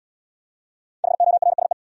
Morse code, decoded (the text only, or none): HORSE